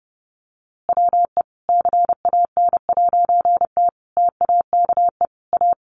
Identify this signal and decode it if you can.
Morse code, heard as a keyed tone, about 28 wpm, 700 Hz. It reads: WE CAN'T TAKE A
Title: A Sound Of Thunder 30wpm